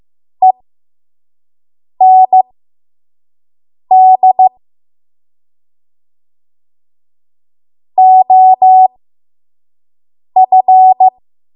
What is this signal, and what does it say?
Morse code, heard as a keyed tone, about 15 wpm, spaced as for 5 wpm, 750 Hz. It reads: END OF
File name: Morse / 05WPM